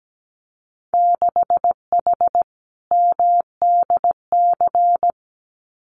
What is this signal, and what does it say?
Morse code, heard as a keyed tone, about 17 wpm, 700 Hz. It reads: 6H MDC
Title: Winter Field Day Exchanges Encore 17wpm